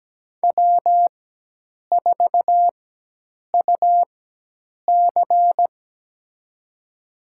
Morse code, heard as a keyed tone, about 17 wpm, spaced as for 8 wpm, 700 Hz. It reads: W4UC
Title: USA Call Signs - 4x Character Spacing - 17wpm